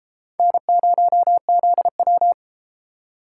Morse code, heard as a keyed tone, about 32 wpm, 700 Hz. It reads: N0ZW